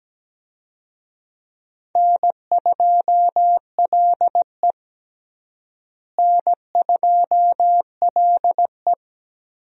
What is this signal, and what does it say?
Morse code, heard as a keyed tone, about 17 wpm, 700 Hz. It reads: N2LE N2LE
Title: USA Call Signs - 4 Characters - Repeated 3x - 17wpm